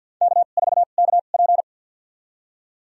Morse code, heard as a keyed tone, about 47 wpm, spaced as for 35 wpm, 700 Hz. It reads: K4KP
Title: USA Call Signs - 2x Character Spacing - 50wpm